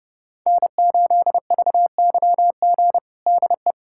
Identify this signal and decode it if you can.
Morse code, heard as a keyed tone, about 31 wpm, 700 Hz. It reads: N8VYG DE
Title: USA Callsign DE Callsign - 5 Characters - 30wpm